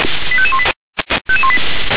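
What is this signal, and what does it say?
Five-tone selective call (ZVEI), heard as quick runs of five tones, tones of about 70 ms. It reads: 85518
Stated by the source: ZVEI five-tone selective call received by SDR.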